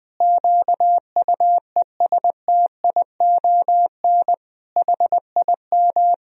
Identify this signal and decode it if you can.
Morse code, heard as a keyed tone, about 20 wpm, 700 Hz. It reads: QUESTION HIM
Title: Sentences from Top 1000 Words - Review Part 3 - 20wpm